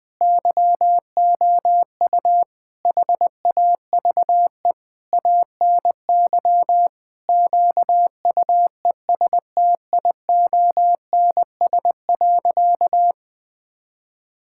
Morse code, heard as a keyed tone, about 20 wpm, 700 Hz. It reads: YOU HAVE ANY QUESTIONS.
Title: Sentences with 7-Word Phrases - 20wpm